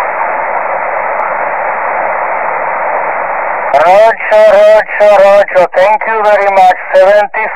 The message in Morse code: none